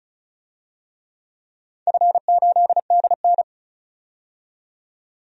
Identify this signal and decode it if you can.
Morse code, heard as a keyed tone, about 36 wpm, 700 Hz. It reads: F8DN
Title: DXCC-100 Call Signs - 4 Characters - 35wpm